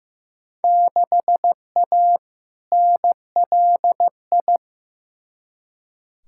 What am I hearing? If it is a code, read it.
Morse code, heard as a keyed tone, about 15 wpm, 700 Hz. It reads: 6A NLI
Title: Field Day Exchanges Encore - 15wpm